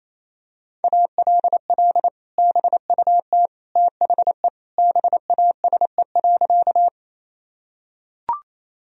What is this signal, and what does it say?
Morse code, heard as a keyed tone, about 28 wpm, 700 Hz. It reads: ALL BUT THE BASE.
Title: Sentences from Top 1000 Words - Review Part 1 - 28wpm